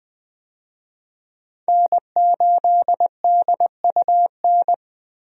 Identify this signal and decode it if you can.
Morse code, heard as a keyed tone, about 20 wpm, 700 Hz. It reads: N8DUN
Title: USA Call Signs - 5 Characters - 20wpm